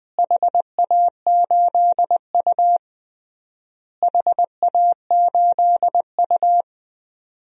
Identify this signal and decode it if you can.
Morse code, heard as a keyed tone, about 21 wpm, 700 Hz. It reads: HA8U HA8U
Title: DXCC-100 Call Signs - 4 Characters - Repeated 3x - 20wpm